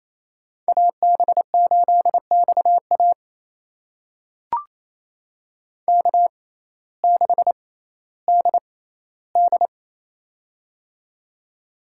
Morse code, heard as a keyed tone, about 28 wpm, spaced as for 15 wpm, 700 Hz. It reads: AB8XA K6DD
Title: USA Call Signs - 6x Character Spacing - 28wpm